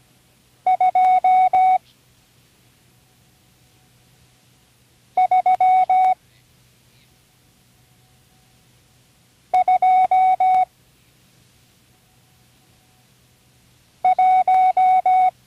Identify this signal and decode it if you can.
Morse code, heard as a keyed tone, about 17 wpm, 740 Hz. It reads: 2 3 2 1
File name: CW Learning Programs / Learning Morse Code (from Tune in the World) / 34 random numbers 1-5